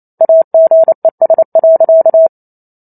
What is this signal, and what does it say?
Morse code, heard as a keyed tone, about 28 wpm, 650 Hz. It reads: AGES.
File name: Chapter 2 - 28 WPM